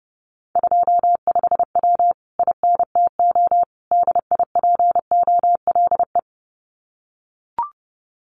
Morse code, heard as a keyed tone, about 30 wpm, 700 Hz. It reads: 25W INTO DIPOLE
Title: Common QSO Phrases Encore 30wpm